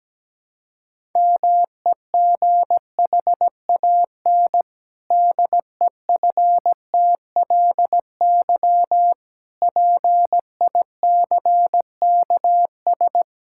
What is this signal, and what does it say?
Morse code, heard as a keyed tone, about 17 wpm, 700 Hz. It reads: MEGHAN DEFTLY PICKS
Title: Pangram Challenge v2 - 17wpm